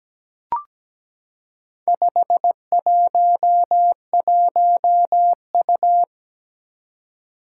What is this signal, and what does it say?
Morse code, heard as a keyed tone, about 17 wpm, 700 Hz. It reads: 511U
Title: Four Challenging Letter-Number Combinations 17wpm